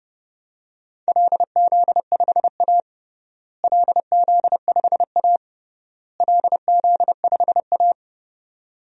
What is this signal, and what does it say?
Morse code, heard as a keyed tone, about 32 wpm, 700 Hz. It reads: LZ5A LZ5A LZ5A
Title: DXCC-100 Call Signs - 4 Characters - Repeated 3x - 30wpm